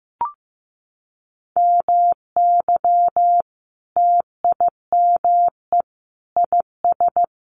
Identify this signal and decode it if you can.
Morse code, heard as a keyed tone, about 15 wpm, 700 Hz. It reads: MY TIME IS
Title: Sentences from Top 1100 Words 15wpm